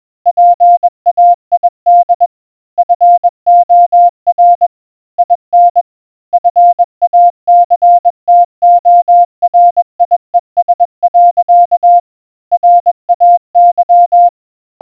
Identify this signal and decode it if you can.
Morse code, heard as a keyed tone, about 21 wpm, 700 Hz. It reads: PAID FOR IN FACTORIES. RAY